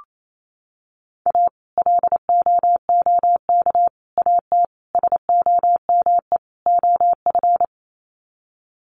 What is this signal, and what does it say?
Morse code, heard as a keyed tone, about 28 wpm, 700 Hz. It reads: A LOOK AT SOME OF
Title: Sets of 5 Words Encore2 28wpm